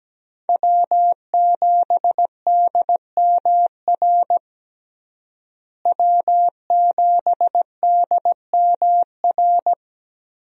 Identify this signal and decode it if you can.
Morse code, heard as a keyed tone, about 17 wpm, 700 Hz. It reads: W7DMR W7DMR
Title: USA Call Signs - 5 Characters - Repeated 3x - 17wpm